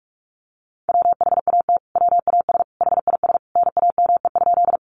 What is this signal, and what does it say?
Morse code, heard as a keyed tone, about 44 wpm, 700 Hz. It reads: WHAT WAS HIS NAME?
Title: Sentences from Top 200 Words 45wpm